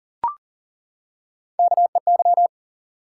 Morse code, heard as a keyed tone, about 42 wpm, 700 Hz. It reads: KEY
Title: Top 180 QSO Elements 40wpm